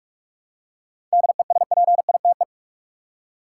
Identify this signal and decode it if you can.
Morse code, heard as a keyed tone, about 43 wpm, 700 Hz. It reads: DESPITE
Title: Seven Letter Words - 45wpm